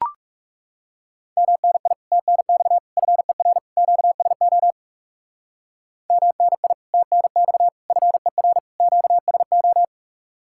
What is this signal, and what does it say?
Morse code, heard as a keyed tone, about 37 wpm, 700 Hz. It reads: MNI TNX FER QSO MNI TNX FER QSO
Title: Speed-Racing Common QSO Phrases - 30wpm